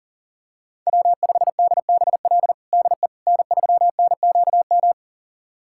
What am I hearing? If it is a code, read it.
Morse code, heard as a keyed tone, about 40 wpm, 700 Hz. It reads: W5DBL DE N3NQM
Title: USA Callsign DE Callsign - 5 Characters - 40wpm